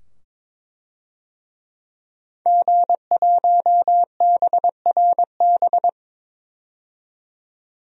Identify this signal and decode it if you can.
Morse code, heard as a keyed tone, about 22 wpm, 700 Hz. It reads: G1BRB
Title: DXCC-100 Call Signs - 5 Characters - 22wpm